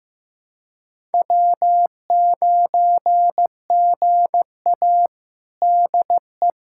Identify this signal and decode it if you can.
Morse code, heard as a keyed tone, about 15 wpm, 700 Hz. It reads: W9GA DE
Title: USA Callsign DE Callsign - 4 Characters - 15wpm